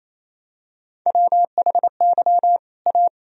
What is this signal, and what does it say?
Morse code, heard as a keyed tone, about 28 wpm, 700 Hz. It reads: WHY A
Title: Sentences from Top 1000 Words - Review Part 2 - 28wpm